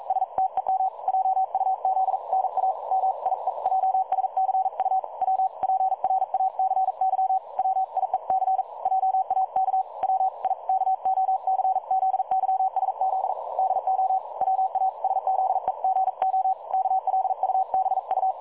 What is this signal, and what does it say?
Morse code, heard as a keyed tone, about 42 wpm, 760 Hz. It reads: STEM JUZ NA MIEJSCU W PRACY WIEC JAK WIKOCZYSZ DO WASZEGO KLUBU
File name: dlazaawansowanych / nagraniazpasma / websdr recording 3565.0kHz